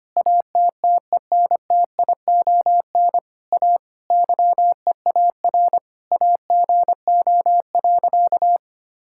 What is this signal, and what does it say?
Morse code, heard as a keyed tone, about 25 wpm, 700 Hz. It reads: ATTENTION A YEAR AGO.
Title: Sentences from Top 1000 Words - Review Part 6 - 25wpm